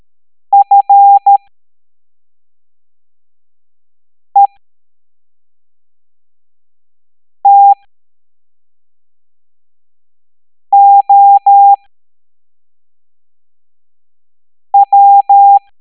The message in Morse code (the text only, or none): FETOW